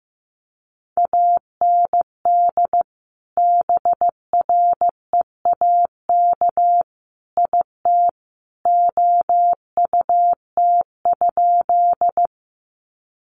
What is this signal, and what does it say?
Morse code, heard as a keyed tone, about 15 wpm, 700 Hz. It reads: AND BREAK IT OUT?
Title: Sentences from Top 900 Words 15wpm